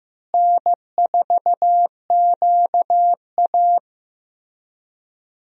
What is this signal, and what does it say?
Morse code, heard as a keyed tone, about 16 wpm, 700 Hz. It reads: N4QA
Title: USA Call Signs - 4 Characters - Repeated 3x - 15wpm